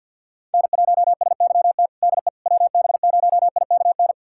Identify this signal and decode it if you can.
Morse code, heard as a keyed tone, about 52 wpm, 700 Hz. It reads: N1SYT DE WB0IKN
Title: USA Callsign DE Callsign 50wpm